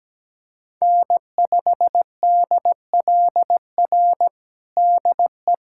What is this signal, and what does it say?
Morse code, heard as a keyed tone, about 17 wpm, 700 Hz. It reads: N5DLR DE
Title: USA Callsign DE Callsign - 5 Characters - 17wpm